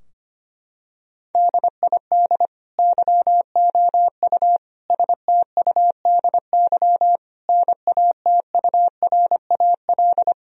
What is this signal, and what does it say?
Morse code, heard as a keyed tone, about 25 wpm, 700 Hz. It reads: DID YOU STUDY NATURAL
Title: Sentences from Top 1000 Words - Review Part 4 - 25wpm